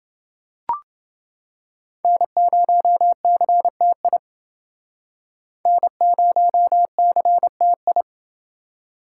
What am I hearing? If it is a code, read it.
Morse code, heard as a keyed tone, about 28 wpm, 700 Hz. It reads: N0CTI N0CTI